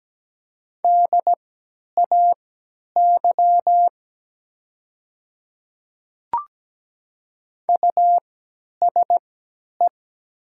Morse code, heard as a keyed tone, about 17 wpm, spaced as for 10 wpm, 700 Hz. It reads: DAY USE
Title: 3x Character Spacing - Top 200 Words - 17wpm